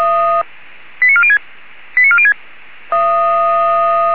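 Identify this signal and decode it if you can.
Five-tone selective call (ZVEI), heard as quick runs of five tones, tones of about 80 ms. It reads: 89406 89406
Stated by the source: Five-tone ZVEI selective call 89300 received by SDR.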